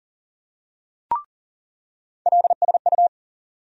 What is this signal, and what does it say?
Morse code, heard as a keyed tone, about 40 wpm, 700 Hz. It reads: LSU
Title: USA Call Sign Postfixes - 3 Letters 40wpm